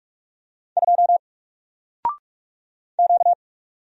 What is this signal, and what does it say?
Morse code, heard as a keyed tone, about 44 wpm, 700 Hz. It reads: 2 Q
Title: Single Letter-Number 45wpm